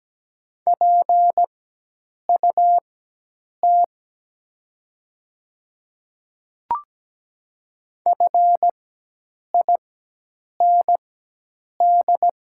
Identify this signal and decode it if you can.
Morse code, heard as a keyed tone, about 17 wpm, spaced as for 8 wpm, 700 Hz. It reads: PUT FIND
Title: 4x Character Spacing - Top 300 Words - 17wpm